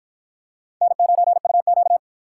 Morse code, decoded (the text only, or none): N9UQ